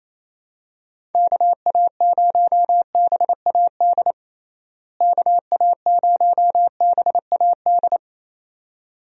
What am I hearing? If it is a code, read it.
Morse code, heard as a keyed tone, about 27 wpm, 700 Hz. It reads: KA0BAD KA0BAD
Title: USA Call Signs - 6 Characters - Repeated 2x - 28wpm